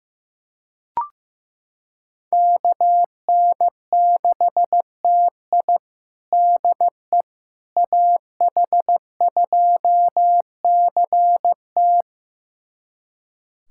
Morse code, heard as a keyed tone, about 15 wpm, 700 Hz. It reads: KN6TI DE AH2CT
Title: USA Callsign DE Callsign - 5 Characters - 15wpm